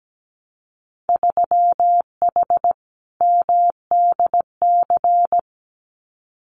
Morse code, heard as a keyed tone, about 17 wpm, 700 Hz. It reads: 3H MDC
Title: Winter Field Day Exchanges Encore 17wpm